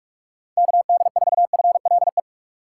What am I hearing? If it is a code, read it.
Morse code, heard as a keyed tone, about 45 wpm, 690 Hz. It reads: KD4FLE
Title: USA Call Signs - 6x Character Spacing - 45wpm